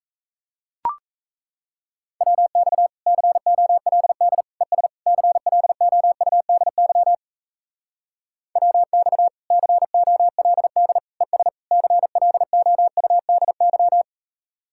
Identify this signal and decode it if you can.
Morse code, heard as a keyed tone, about 38 wpm, 700 Hz. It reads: WX COLD ES CLOUDY WX COLD ES CLOUDY